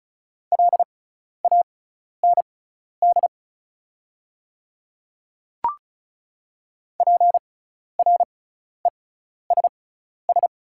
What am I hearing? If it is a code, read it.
Morse code, heard as a keyed tone, about 36 wpm, spaced as for 12 wpm, 690 Hz. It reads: LAND PRESS
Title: 6x Character Spacing - Top 300 Words - 35wpm